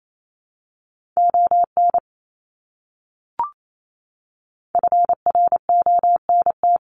Morse code, heard as a keyed tone, about 28 wpm, 700 Hz. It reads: ON FRONT
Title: Top 500 Words Repeated 28wpm